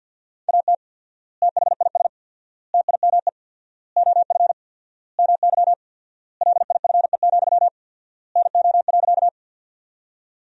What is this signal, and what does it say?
Morse code, heard as a keyed tone, about 48 wpm, 700 Hz. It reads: AT THIS TIME OF MY LIFE, NO.
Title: 4x Word-Spacing - Sentences from Top 300 Words - 50wpm